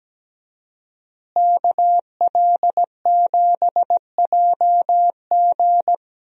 Morse code, heard as a keyed tone, about 17 wpm, 700 Hz. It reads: KL7JG
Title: DXCC-100 Call Signs - 5 Characters - Repeated 2x - 17wpm